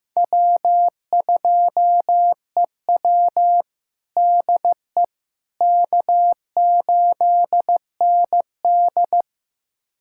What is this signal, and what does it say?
Morse code, heard as a keyed tone, about 15 wpm, 700 Hz. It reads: W2EW DE K8ND
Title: USA Callsign DE Callsign - 4 Characters - 15wpm